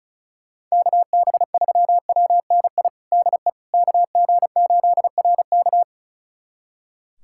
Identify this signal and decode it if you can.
Morse code, heard as a keyed tone, about 36 wpm, 700 Hz. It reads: KB3WNI DE KG8RK
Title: USA Callsign DE Callsign 35wpm